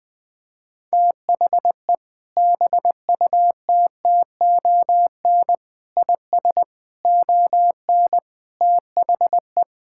Morse code, heard as a keyed tone, about 20 wpm, 700 Hz. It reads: THE BUTTON IS ON THE
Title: Sentences with 6-Word Phrases - 20wpm